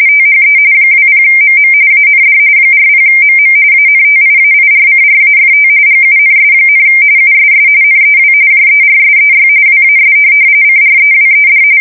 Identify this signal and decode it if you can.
RTTY, about 45.45 baud. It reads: THE 20MM AND 40MM ANTIAIRCRAFT GUNS, USS NEW JERSEY WAS CREWED
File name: fsk170 / USS-New-Jersey